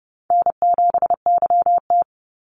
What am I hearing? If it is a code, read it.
Morse code, heard as a keyed tone, about 30 wpm, 700 Hz. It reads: N7YT
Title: Call Signs Repeated 3x - 30wpm